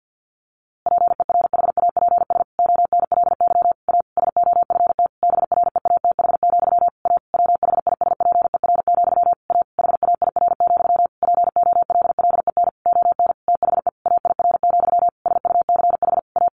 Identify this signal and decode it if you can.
Morse code, heard as a keyed tone, about 50 wpm, 700 Hz. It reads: PERHAPS ONLY A SOFT BREATH, A WHISPER, A HAIR, POLLEN ON THE AIR, SUCH A
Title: A Sound Of Thunder 50wpm